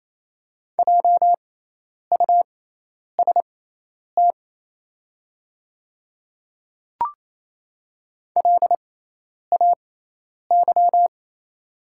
Morse code, heard as a keyed tone, about 28 wpm, spaced as for 10 wpm, 700 Hz. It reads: JUST LAY
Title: 6x Character Spacing - Top 500 Words - 28wpm